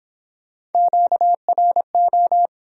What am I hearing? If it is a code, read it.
Morse code, heard as a keyed tone, about 30 wpm, 700 Hz. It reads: QRO